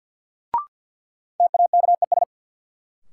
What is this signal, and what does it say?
Morse code, heard as a keyed tone, about 53 wpm, 700 Hz. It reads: TAKES
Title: Sets of 5 Words Encore - New Word Drill - 50wpm